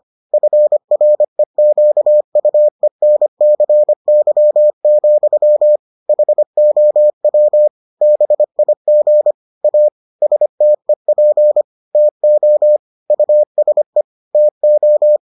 Morse code, heard as a keyed tone, about 25 wpm, 600 Hz. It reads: FREQUENCY, HOW BIG A STEP TO USE TO